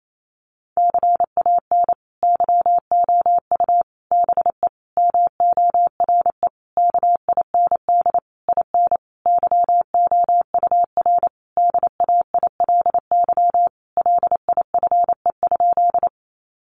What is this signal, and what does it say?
Morse code, heard as a keyed tone, about 28 wpm, 700 Hz. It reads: CAN YOU BE MORE KIND IN YOUR DAILY LIFE?
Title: Sentences from Top 1100 Words 28wpm